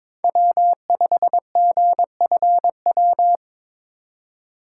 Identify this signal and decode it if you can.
Morse code, heard as a keyed tone, about 22 wpm, 700 Hz. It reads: W5GFW
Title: USA Call Signs - 5 Characters - 4x Character Spacing - 22wpm